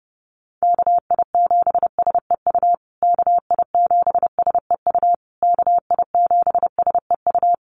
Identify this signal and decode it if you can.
Morse code, heard as a keyed tone, about 28 wpm, 700 Hz. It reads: KI7SEU KI7SEU KI7SEU
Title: USA Call Signs Repeated 3x 30wpm